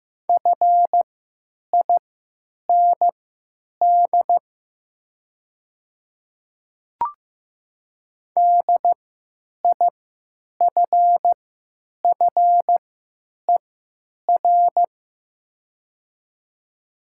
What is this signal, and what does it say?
Morse code, heard as a keyed tone, about 15 wpm, spaced as for 9 wpm, 700 Hz. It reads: FIND DIFFER
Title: 3x Character Spacing - Top 200 Words - 15wpm